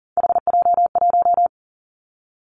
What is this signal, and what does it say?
Morse code, heard as a keyed tone, about 42 wpm, 690 Hz. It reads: HJ1
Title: Three Challenging Letter-Number Combinations 40wpm